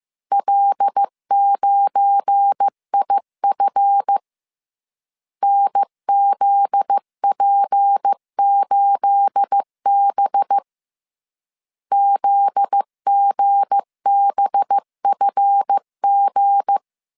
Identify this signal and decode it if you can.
Morse code, heard as a keyed tone, about 15 wpm, 800 Hz. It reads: L9IF NZP8B ZGBFG
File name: CW / DET / 1 9 1500C S A 14wpm 3222